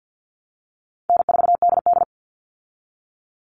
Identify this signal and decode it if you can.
Morse code, heard as a keyed tone, about 50 wpm, 700 Hz. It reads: N4DD
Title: Easy Call Signs Repeated 2x - 50wpm